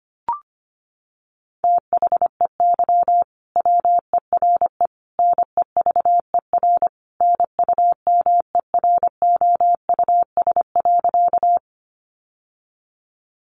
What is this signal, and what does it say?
Morse code, heard as a keyed tone, about 25 wpm, 700 Hz. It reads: THEY WERE NEVER NUMEROUS.
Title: Sentences from Top 1400 Words 25wpm